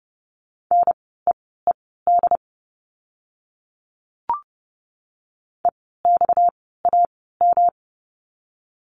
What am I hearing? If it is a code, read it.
Morse code, heard as a keyed tone, about 30 wpm, spaced as for 17 wpm, 700 Hz. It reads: NEED EXAM
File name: course / 30:10wpm / 230 Lesson 37f - Letter x - Words 30wpm